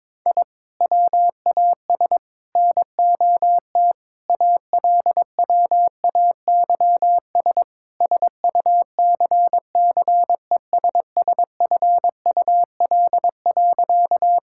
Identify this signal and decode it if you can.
Morse code, heard as a keyed tone, about 22 wpm, 700 Hz. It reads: I WAS NOT ALWAYS SUCCESSFUL.
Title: Sentences from Top 1000 Words - Review Part 6 - 22wpm